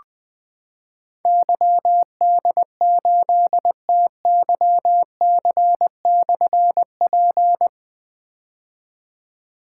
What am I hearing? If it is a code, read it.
Morse code, heard as a keyed tone, about 20 wpm, 700 Hz. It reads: YD8TYC/P
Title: Difficult Call Signs 20wpm